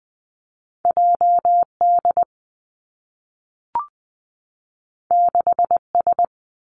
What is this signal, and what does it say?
Morse code, heard as a keyed tone, about 20 wpm, 700 Hz. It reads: J D 6 S
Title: Two Challenging Letter-Number Combinations 20wpm